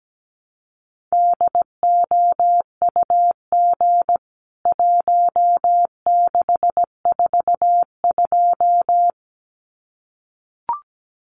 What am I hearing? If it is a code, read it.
Morse code, heard as a keyed tone, about 17 wpm, 700 Hz. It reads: DOUG 1642
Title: CWT Exchange 17wpm